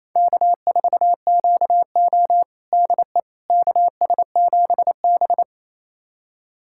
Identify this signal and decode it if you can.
Morse code, heard as a keyed tone, about 28 wpm, 700 Hz. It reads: K4QO DE KS7B
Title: USA Callsign DE Callsign - 4 Characters - 28wpm